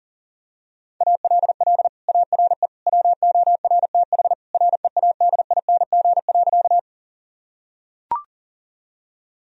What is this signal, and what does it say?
Morse code, heard as a keyed tone, about 40 wpm, 700 Hz. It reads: ALL ARE WORTH READING.
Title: Sentences from Top 1000 Words - Review Part 3 - 40wpm